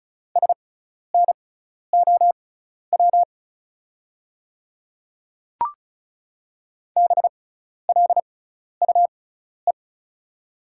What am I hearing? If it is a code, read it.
Morse code, heard as a keyed tone, about 36 wpm, spaced as for 12 wpm, 690 Hz. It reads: SNOW BLUE
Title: 6x Character Spacing - Top 500 Words - 35wpm